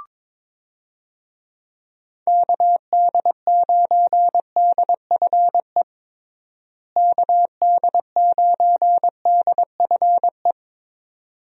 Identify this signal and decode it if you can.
Morse code, heard as a keyed tone, about 22 wpm, 700 Hz. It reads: KD9DFE KD9DFE
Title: USA Call Signs - 6 Characters - Repeated 2x - 22wpm